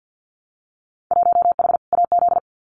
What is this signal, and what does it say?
Morse code, heard as a keyed tone, about 50 wpm, 700 Hz. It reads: 1H AZ
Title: Winter Field Day Exchanges 50wpm